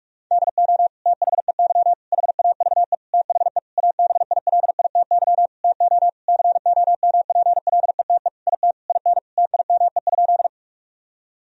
Morse code, heard as a keyed tone, about 41 wpm, 700 Hz. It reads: DO THEY HAVE THE ABILITY TO COMPLETE IT IN TIME?
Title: Sentences from Top 1000 Words - Review Part 3 - 45wpm